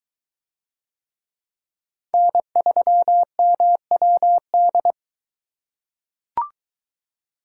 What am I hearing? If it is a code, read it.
Morse code, heard as a keyed tone, about 23 wpm, 700 Hz. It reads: N3MWD